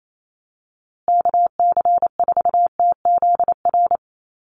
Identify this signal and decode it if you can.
Morse code, heard as a keyed tone, about 28 wpm, 700 Hz. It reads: KC4TZR
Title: USA Call Signs Repeated 3x 28wpm